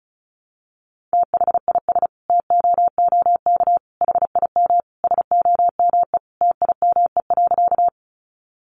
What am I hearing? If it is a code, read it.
Morse code, heard as a keyed tone, about 34 wpm, 700 Hz. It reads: THIS TOOK HIM SOME TIME.
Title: Sentences from Top 300 Words 35wpm